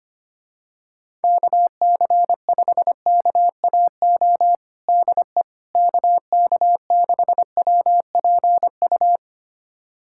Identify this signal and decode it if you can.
Morse code, heard as a keyed tone, about 25 wpm, 700 Hz. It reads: KC5KAO DE KK6WPU
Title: USA Callsign DE Callsign 25wpm